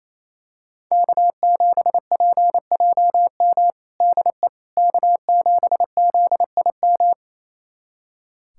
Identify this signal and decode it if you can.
Morse code, heard as a keyed tone, about 28 wpm, 700 Hz. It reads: K7PJM DE K7ZIM
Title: USA Callsign DE Callsign - 5 Characters - 28wpm